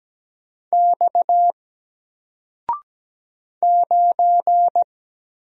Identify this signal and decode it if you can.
Morse code, heard as a keyed tone, about 17 wpm, 700 Hz. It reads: X 9
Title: Single Letter-Number Alt 17wpm